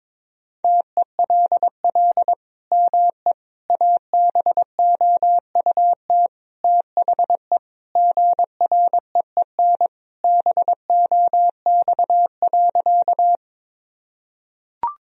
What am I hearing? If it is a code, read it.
Morse code, heard as a keyed tone, about 22 wpm, 700 Hz. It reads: TELL ME ABOUT THE GREEN BOX.
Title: Sentences from Top 1000 Words - Review Part 2 - 22wpm